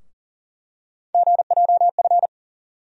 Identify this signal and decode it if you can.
Morse code, heard as a keyed tone, about 40 wpm, 700 Hz. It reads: GJF
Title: USA Call Sign Postfixes - 3 Letters 40wpm